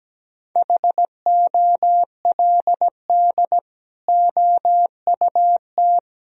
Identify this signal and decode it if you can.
Morse code, heard as a keyed tone, about 17 wpm, 700 Hz. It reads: HOLD OUT
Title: Sentences from Top 1000 Words - Review Part 2 - 17wpm